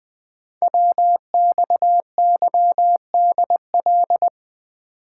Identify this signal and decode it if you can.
Morse code, heard as a keyed tone, about 20 wpm, 700 Hz. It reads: WXYDL
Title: Code Groups - 5 Letters - 20wpm